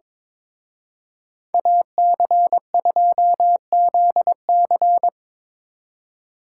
Morse code, heard as a keyed tone, about 26 wpm, 700 Hz. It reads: AC2ZC
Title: Speed Racing - USA Call Signs 22wpm